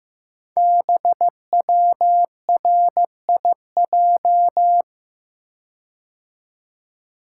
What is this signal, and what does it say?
Morse code, heard as a keyed tone, about 15 wpm, 700 Hz. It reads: BWRIJ
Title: Code Groups - 5 Letters - 15wpm